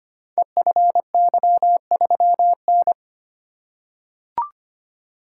Morse code, heard as a keyed tone, about 25 wpm, 700 Hz. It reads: EFY3N
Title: Code Groups - 5 Letter:Numbers - 25wpm